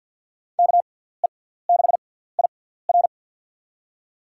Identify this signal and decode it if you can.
Morse code, heard as a keyed tone, about 47 wpm, spaced as for 17 wpm, 700 Hz. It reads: KE6IR
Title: USA Call Signs - 5 Characters - 6x Character Spacing - 50wpm